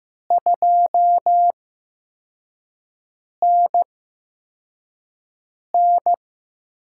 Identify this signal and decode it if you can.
Morse code, heard as a keyed tone, about 15 wpm, spaced as for 4 wpm, 700 Hz. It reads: 2NN
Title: USA Call Signs - 4 Characters - 8x Character Spacing - 15wpm